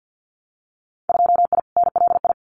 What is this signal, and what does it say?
Morse code, heard as a keyed tone, about 50 wpm, 700 Hz. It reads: 2I NLI
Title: Winter Field Day Exchanges 50wpm